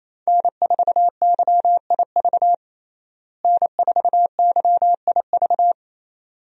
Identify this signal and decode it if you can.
Morse code, heard as a keyed tone, about 28 wpm, 700 Hz. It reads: N4YIV N4YIV
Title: USA Call Signs - 5 Characters - Repeated 2x - 28wpm